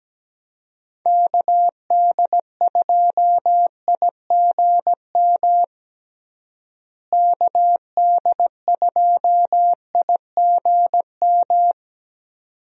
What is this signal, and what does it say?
Morse code, heard as a keyed tone, about 17 wpm, 700 Hz. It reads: KD2IGM KD2IGM
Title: USA Call Signs - 6 Characters - Repeated 2x - 17wpm